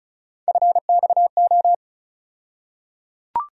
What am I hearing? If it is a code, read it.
Morse code, heard as a keyed tone, about 36 wpm, 700 Hz. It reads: FXO
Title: USA Call Sign Postfixes - 3 Letters 35wpm